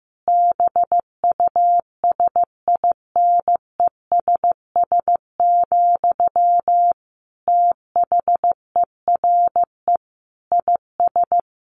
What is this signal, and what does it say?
Morse code, heard as a keyed tone, about 15 wpm, 700 Hz. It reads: BUSINESS, THERE IS
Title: Sentences from Top 1400 Words 15wpm